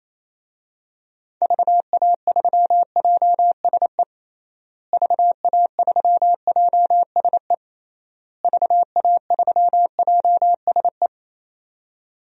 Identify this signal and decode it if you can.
Morse code, heard as a keyed tone, about 28 wpm, 700 Hz. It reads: VA3JSE VA3JSE VA3JSE
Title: DXCC-100 Call Signs - 6 Characters - Repeated 3x - 28wpm